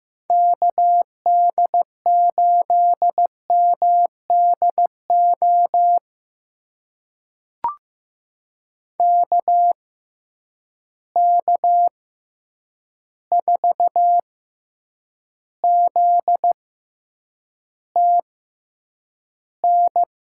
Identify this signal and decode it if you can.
Morse code, heard as a keyed tone, about 15 wpm, spaced as for 8 wpm, 700 Hz. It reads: KD8MDO KK4ZTN
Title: USA Call Signs - 6 Characters - 6x Character Spacing - 15wpm